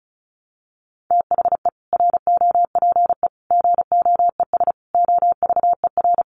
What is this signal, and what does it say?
Morse code, heard as a keyed tone, about 34 wpm, 700 Hz. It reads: THE ROPE GOES OVER
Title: Easy Phrases 35wpm